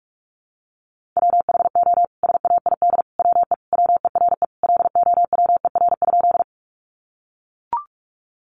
Morse code, heard as a keyed tone, about 44 wpm, 700 Hz. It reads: WHO SAID WE WERE LOWER?
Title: Sentences from Top 700 Words 45wpm